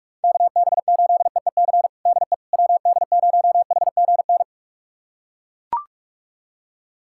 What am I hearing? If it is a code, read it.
Morse code, heard as a keyed tone, about 41 wpm, 700 Hz. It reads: KB8EEC DE WD0HGN
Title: USA Callsign DE Callsign - 6 Characters - 45wpm